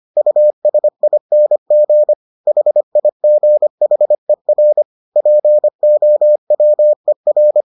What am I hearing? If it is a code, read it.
Morse code, heard as a keyed tone, about 25 wpm, 600 Hz. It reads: USING HIGHER POWER